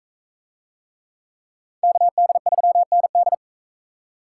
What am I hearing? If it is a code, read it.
Morse code, heard as a keyed tone, about 44 wpm, 700 Hz. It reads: KD3ND